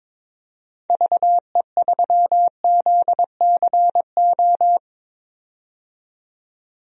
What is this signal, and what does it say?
Morse code, heard as a keyed tone, about 22 wpm, 700 Hz. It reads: VE3ZCO
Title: DXCC-100 Call Signs - 6 Characters - 22wpm